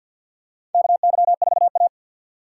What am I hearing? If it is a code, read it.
Morse code, heard as a keyed tone, about 47 wpm, 690 Hz. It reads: KY4A
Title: USA Call Signs - 4 Characters - 6x Character Spacing - 50wpm